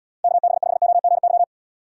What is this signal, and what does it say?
Morse code, heard as a keyed tone, about 78 wpm, 700 Hz. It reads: KB5OGY